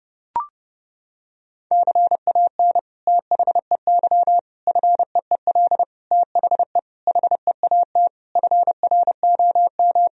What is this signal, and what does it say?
Morse code, heard as a keyed tone, about 30 wpm, 700 Hz. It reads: CAN THEY FEEL THE HEAT FROM
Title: Sentences from Top 1000 Words - Review Part 3 - 30wpm